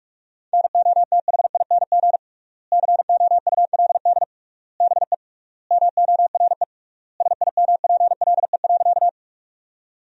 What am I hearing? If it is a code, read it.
Morse code, heard as a keyed tone, about 46 wpm, 700 Hz. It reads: NOTHING COULD BE MORE SIMPLE.
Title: 3x Word-Spacing - Sentences from Top 500 Words - 45wpm